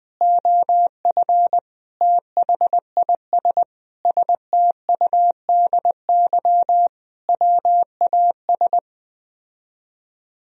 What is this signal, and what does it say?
Morse code, heard as a keyed tone, about 20 wpm, 700 Hz. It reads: OF THIS STUDY WAS